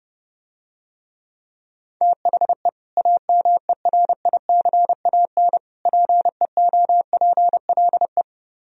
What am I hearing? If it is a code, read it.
Morse code, heard as a keyed tone, about 30 wpm, 700 Hz. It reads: THE AMERICAN PEOPLE